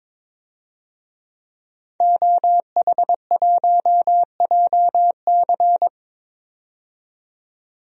Morse code, heard as a keyed tone, about 22 wpm, 700 Hz. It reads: OH1JC
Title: DXCC-100 Call Signs - 5 Characters - 22wpm